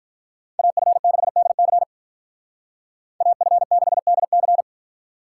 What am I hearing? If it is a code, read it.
Morse code, heard as a keyed tone, about 43 wpm, 700 Hz. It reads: AF6DC AF6DC
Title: Speed Racing - USA 2x2 Call Signs 35wpm